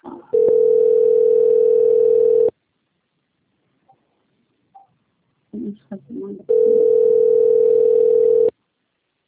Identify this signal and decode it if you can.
Ringback tone (North America), 440 + 480 Hz, with speech.
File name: out- - - - - .37090